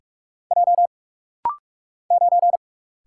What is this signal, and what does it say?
Morse code, heard as a keyed tone, about 44 wpm, 700 Hz. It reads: J 9
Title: Single Letter-Number Alt - Rapid-Fire 45wpm